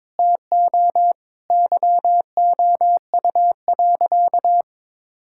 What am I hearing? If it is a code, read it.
Morse code, heard as a keyed tone, about 21 wpm, 700 Hz. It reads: TO YOU.
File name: practice / 22wpm / Sentences with 4-Word Phrases - 22wpm